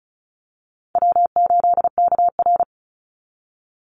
Morse code, heard as a keyed tone, about 34 wpm, 700 Hz. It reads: W8KR
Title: Easy Call Signs Repeated 2x - 35wpm